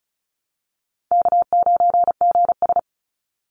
Morse code, heard as a keyed tone, about 34 wpm, 700 Hz. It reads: K9GS
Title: Call Signs Repeated 3x - 35wpm